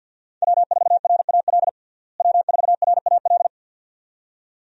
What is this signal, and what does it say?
Morse code, heard as a keyed tone, about 47 wpm, 700 Hz. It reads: W4RAL W4RAL
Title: USA Call Signs - 5 Characters - Repeated 2x - 50wpm